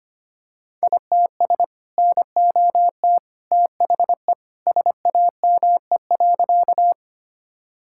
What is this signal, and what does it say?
Morse code, heard as a keyed tone, about 25 wpm, 700 Hz. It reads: ITS NOT THE SAME.
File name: practice / 25wpm / Sentences with 2-Word Phrases - 25wpm